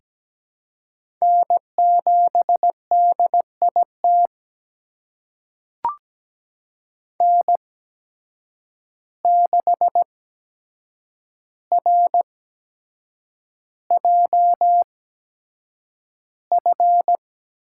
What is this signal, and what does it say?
Morse code, heard as a keyed tone, about 17 wpm, spaced as for 7 wpm, 700 Hz. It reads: N7DIT N6RJF
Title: USA Call Signs - 5 Characters - 8x Character Spacing - 17wpm